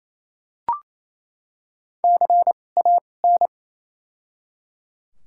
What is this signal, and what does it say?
Morse code, heard as a keyed tone, about 28 wpm, 700 Hz. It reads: C A N
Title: 2x Character Spacing - Top 200 Words - 28wpm